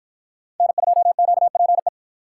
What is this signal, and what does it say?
Morse code, heard as a keyed tone, about 52 wpm, 700 Hz. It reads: N2QPE